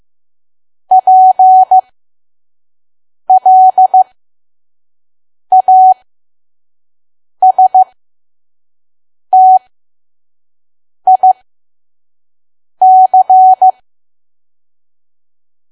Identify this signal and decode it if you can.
Morse code, heard as a keyed tone, about 15 wpm, spaced as for 5 wpm, 750 Hz. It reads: PLASTIC
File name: Archive / 5 WPM files / PM